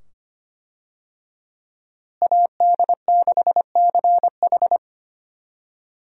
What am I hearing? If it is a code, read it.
Morse code, heard as a keyed tone, about 25 wpm, 700 Hz. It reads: AD6CH
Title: USA Call Signs - 5 Characters - Repeated 2x - 25wpm